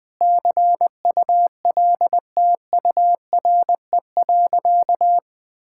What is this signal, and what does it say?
Morse code, heard as a keyed tone, about 20 wpm, 700 Hz. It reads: CULTURE.
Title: Sentences from Top 1000 Words - Review Part 6 - 20wpm